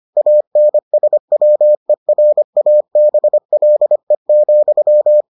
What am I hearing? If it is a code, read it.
Morse code, heard as a keyed tone, about 25 wpm, 600 Hz. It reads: ANSWERABLE,